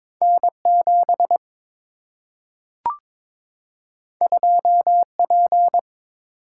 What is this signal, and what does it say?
Morse code, heard as a keyed tone, about 22 wpm, 700 Hz. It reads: N 7 2 P
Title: Single-Letter Single-Number 22wpm